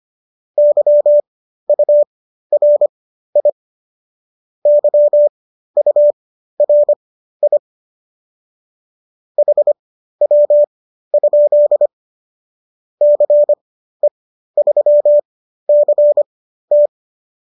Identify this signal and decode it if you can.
Morse code, heard as a keyed tone, about 25 wpm, spaced as for 13 wpm, 590 Hz. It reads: YURI YURI HW? CE3CT